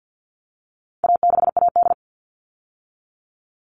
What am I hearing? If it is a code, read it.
Morse code, heard as a keyed tone, about 50 wpm, 700 Hz. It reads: A6AD
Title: All Cut Numbers - 4-Digit 50wpm